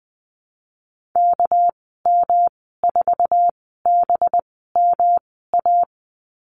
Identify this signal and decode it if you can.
Morse code, heard as a keyed tone, about 20 wpm, spaced as for 14 wpm, 700 Hz. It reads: KM4BMA
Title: 213 Lesson 34g - Number 8 - Call Signs 20wpm